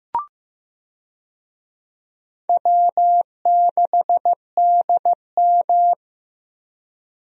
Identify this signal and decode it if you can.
Morse code, heard as a keyed tone, about 15 wpm, 700 Hz. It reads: W6DM
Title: USA Call Signs - 4 Characters - Repeated 3x - 15wpm